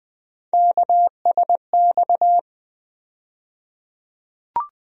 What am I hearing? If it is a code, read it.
Morse code, heard as a keyed tone, about 20 wpm, 700 Hz. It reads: KSX
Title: USA Call Sign Postfixes - 3 Letters 20wpm